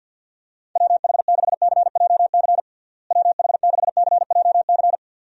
Morse code, heard as a keyed tone, about 52 wpm, 700 Hz. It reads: WH6CJC WH6CJC
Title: USA Call Signs - 6 Characters - Repeated 3x - 50wpm